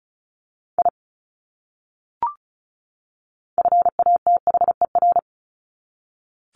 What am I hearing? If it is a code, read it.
Morse code, heard as a keyed tone, about 34 wpm, 700 Hz. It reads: I FATHER
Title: Top 200 Words 35wpm